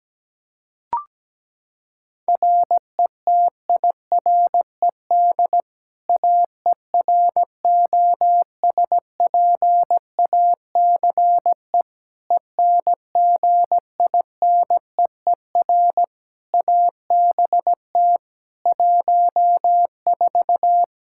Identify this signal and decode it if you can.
Morse code, heard as a keyed tone, about 17 wpm, 700 Hz. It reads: RETIRED AEROSPACE ENGINEER ABT 14
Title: K1USN SST QSO Practice 17wpm